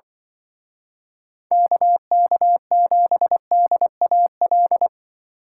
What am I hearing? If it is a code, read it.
Morse code, heard as a keyed tone, about 24 wpm, 700 Hz. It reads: KK7DAL
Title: Speed Racing - USA 2x3 Call Signs 20wpm